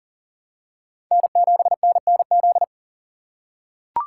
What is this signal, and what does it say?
Morse code, heard as a keyed tone, about 40 wpm, 700 Hz. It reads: N7NNZ
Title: USA Call Signs - 5 Characters - 6x Character Spacing - 40wpm